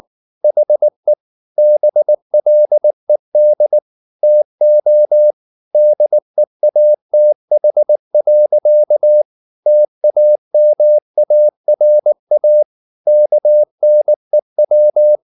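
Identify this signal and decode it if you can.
Morse code, heard as a keyed tone, about 19 wpm, 600 Hz. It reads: HE BLED TO DEATH. TAMARA KNEW
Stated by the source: Morse code at 19 wpm from CW med Gnister, reading Ken Follett's novel Never in English.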